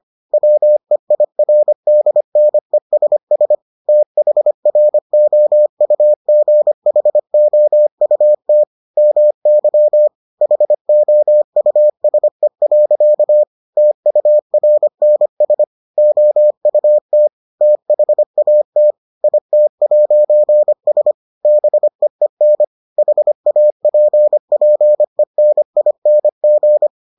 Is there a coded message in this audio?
Yes: Morse code from 0.3 to 26.9 s, heard as a keyed tone, about 25 wpm, 600 Hz. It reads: WEIRDNESS THROUGHOUT MY HOUSE. TURNS OUT THAT IT'S BEEN HAPPENING